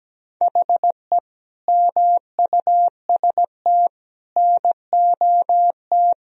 Morse code, heard as a keyed tone, about 17 wpm, 700 Hz. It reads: HE MUST NOT
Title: Sentences from Top 1000 Words - Review Part 2 - 17wpm